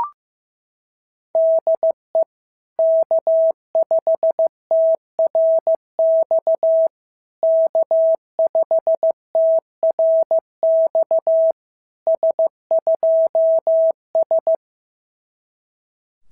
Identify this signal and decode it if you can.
Morse code, heard as a keyed tone, about 15 wpm, 690 Hz. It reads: DE K5TRX K5TRX S2S
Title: SOTA QSO Practice 15wpm